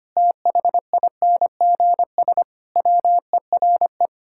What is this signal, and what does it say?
Morse code, heard as a keyed tone, about 25 wpm, 700 Hz. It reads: THINGS WERE
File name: practice / 25wpm / Sentences with 4-Word Phrases - 25wpm